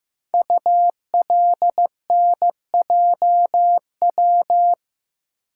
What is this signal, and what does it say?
Morse code, heard as a keyed tone, about 15 wpm, 700 Hz. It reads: ULNJW
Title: Code Groups - 5 Letters - 15wpm